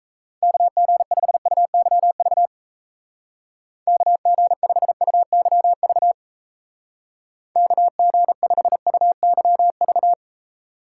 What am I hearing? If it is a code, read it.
Morse code, heard as a keyed tone, about 34 wpm, 700 Hz. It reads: KG5UYV KG5UYV KG5UYV
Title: Speed Racing - USA 2x3 Call Signs 28wpm